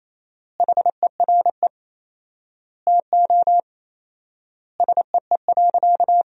Morse code, heard as a keyed tone, about 28 wpm, 700 Hz. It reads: HERE TO SEE.
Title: 4x Word-Spacing - Sentences from Top 200 Words - 28wpm